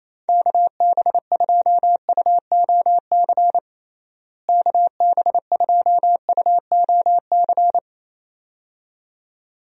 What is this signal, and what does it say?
Morse code, heard as a keyed tone, about 28 wpm, 700 Hz. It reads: KB2UOC KB2UOC
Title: USA Call Signs - 6 Characters - Repeated 2x - 28wpm